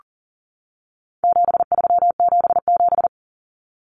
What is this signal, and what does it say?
Morse code, heard as a keyed tone, about 40 wpm, 700 Hz. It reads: 7377
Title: Common Cut Numbers - 4-Digit 40wpm